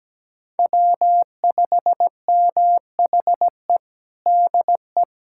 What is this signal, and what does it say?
Morse code, heard as a keyed tone, about 17 wpm, 700 Hz. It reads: W5MHE DE
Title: USA Callsign DE Callsign - 5 Characters - 17wpm